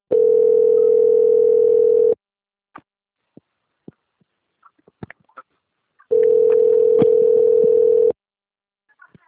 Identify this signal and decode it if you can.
Ringback tone (North America), 440 + 480 Hz.